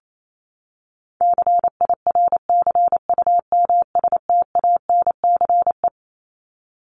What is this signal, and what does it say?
Morse code, heard as a keyed tone, about 28 wpm, 700 Hz. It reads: CIRCUMSTANCE
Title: ES:S Suffix - Top 100 Words with Base Words 28wpm